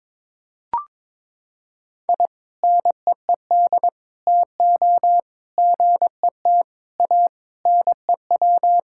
Morse code, heard as a keyed tone, about 22 wpm, 700 Hz. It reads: I NEED TO GET A NEW
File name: practice / 22wpm / Sentences with 3-Word Phrases - 22wpm